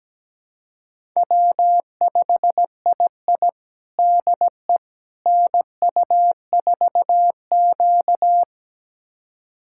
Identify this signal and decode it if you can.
Morse code, heard as a keyed tone, about 17 wpm, 700 Hz. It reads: W5II DE NU4Q
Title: USA Callsign DE Callsign - 4 Characters - 17wpm